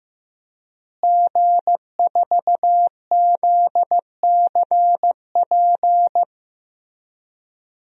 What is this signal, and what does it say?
Morse code, heard as a keyed tone, about 15 wpm, 700 Hz. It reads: G4ZCP
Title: DXCC-100 Call Signs - 5 Characters - Repeated 2x - 15wpm